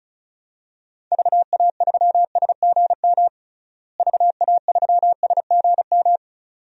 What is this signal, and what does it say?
Morse code, heard as a keyed tone, about 36 wpm, 700 Hz. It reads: VA3SGM VA3SGM
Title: DXCC-100 Call Signs - 6 Characters - Repeated 2x - 35wpm